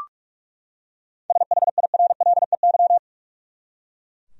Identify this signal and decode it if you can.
Morse code, heard as a keyed tone, about 42 wpm, 700 Hz. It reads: SHIRLEY
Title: USA Common Names - 45wpm